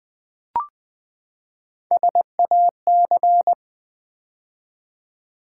Morse code, heard as a keyed tone, about 20 wpm, 700 Hz. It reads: SAC
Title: Build A Word - 3 Letters At A Time - 9-Letter Words - 20wpm